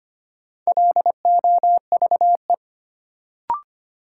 Morse code, heard as a keyed tone, about 25 wpm, 700 Hz. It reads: LOVE
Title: Sets of 5 Words Encore - New Word Drill - 25wpm